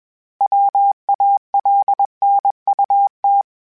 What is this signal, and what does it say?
Morse code, heard as a keyed tone, about 21 wpm, 800 Hz. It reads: WALNUT
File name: fromSMs / FromSMs mp3s / Cities@20